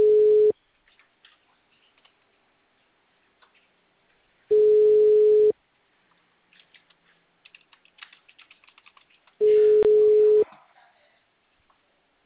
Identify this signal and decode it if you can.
Ringback tone (Europe, 425 Hz).
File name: out- -2012- - - .554